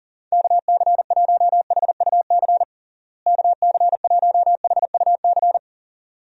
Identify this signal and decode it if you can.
Morse code, heard as a keyed tone, about 40 wpm, 700 Hz. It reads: KC1HUC KC1HUC
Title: USA Call Signs - 6 Characters - Repeated 2x - 40wpm